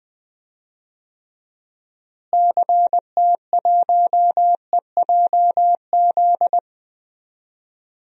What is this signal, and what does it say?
Morse code, heard as a keyed tone, about 20 wpm, 700 Hz. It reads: CT1EJZ
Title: DXCC-100 Call Signs - 6 Characters - Repeated 2x - 20wpm